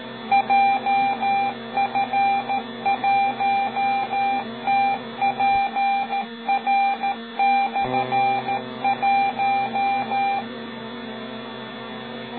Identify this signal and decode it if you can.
Morse code, heard as a keyed tone, about 13 wpm, 790 Hz, with noise. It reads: JF1TPR/1
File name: Beacon / 24GidJF1TPR M